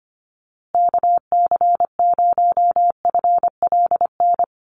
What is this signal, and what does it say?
Morse code, heard as a keyed tone, about 25 wpm, 700 Hz. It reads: KC0FLN
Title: USA Call Signs Repeated 3x 25wpm